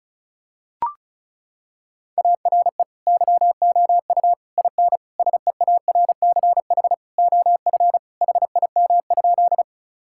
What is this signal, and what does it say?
Morse code, heard as a keyed tone, about 36 wpm, 700 Hz. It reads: ARE YOU IN SEARCH OF HIM?
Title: Sentences from Top 1000 Words - Review Part 5 - 35wpm